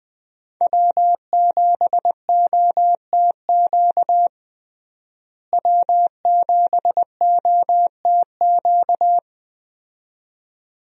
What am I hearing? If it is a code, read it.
Morse code, heard as a keyed tone, about 20 wpm, 700 Hz. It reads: W7OTQ W7OTQ
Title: USA Call Signs - 5 Characters - Repeated 2x - 20wpm